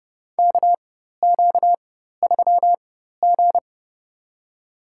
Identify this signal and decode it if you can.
Morse code, heard as a keyed tone, about 31 wpm, spaced as for 14 wpm, 700 Hz. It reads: KQ3G
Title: USA Call Signs - 4x Character Spacing - 30wpm